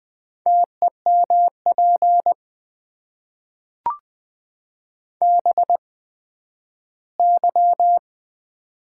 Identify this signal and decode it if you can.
Morse code, heard as a keyed tone, about 20 wpm, spaced as for 12 wpm, 700 Hz. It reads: TEMP BY
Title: Top 90 QSO Elements - 8x Character Spacing - 20wpm